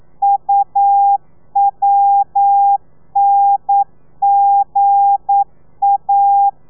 Morse code, heard as a keyed tone, about 9 wpm, 800 Hz. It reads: UWNGA
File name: telegrafia / i2tfj / Lez8 40